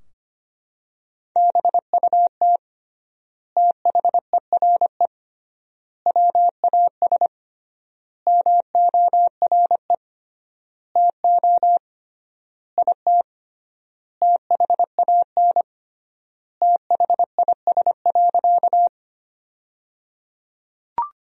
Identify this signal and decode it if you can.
Morse code, heard as a keyed tone, about 25 wpm, 700 Hz. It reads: BUT THERE WAS MORE TO IT THAN THIS.
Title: 3x Word-Spacing - Sentences from Top 100 Words - 25wpm